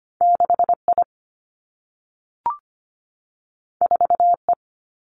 Morse code, heard as a keyed tone, about 25 wpm, 700 Hz. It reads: 6I 4E
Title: Two Challenging Letter-Number Combinations 25wpm